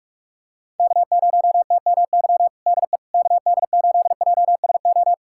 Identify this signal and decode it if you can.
Morse code, heard as a keyed tone, about 41 wpm, 700 Hz. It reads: K0TMY DE KD8JSO
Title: USA Callsign DE Callsign 45wpm